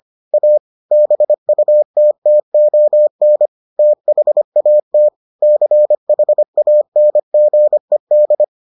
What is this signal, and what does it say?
Morse code, heard as a keyed tone, about 25 wpm, 600 Hz. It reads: A BUTTON THAT CHANGED